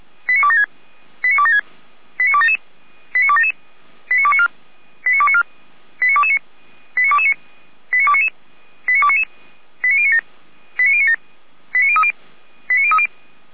Five-tone selective call (ZVEI), heard as quick runs of five tones, tones of about 70 ms. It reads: 89276 89276 89277 89277 89294 89294 89229 89229 89299 89299 89097 89097 89030 89030